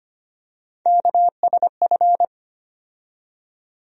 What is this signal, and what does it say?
Morse code, heard as a keyed tone, about 25 wpm, 700 Hz. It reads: KSF
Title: USA Call Sign Postfixes - 3 Letters 25wpm